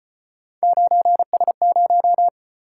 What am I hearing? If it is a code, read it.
Morse code, heard as a keyed tone, about 36 wpm, 700 Hz. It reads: 9S0